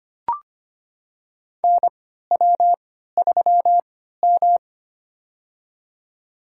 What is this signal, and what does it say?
Morse code, heard as a keyed tone, about 25 wpm, spaced as for 14 wpm, 700 Hz. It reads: NW3M
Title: USA Call Signs - 4 Characters - 3x Character Spacing - 25wpm